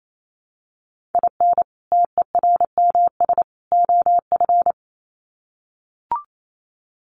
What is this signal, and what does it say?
Morse code, heard as a keyed tone, about 28 wpm, 700 Hz. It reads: IN TERMS OF
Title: Sets of 3 Words 28wpm